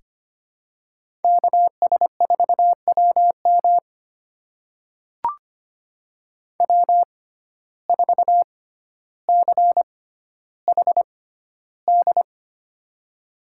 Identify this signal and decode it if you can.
Morse code, heard as a keyed tone, about 25 wpm, spaced as for 13 wpm, 700 Hz. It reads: KS4WM W4CHD
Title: USA Call Signs - 5 Characters - 6x Character Spacing - 25wpm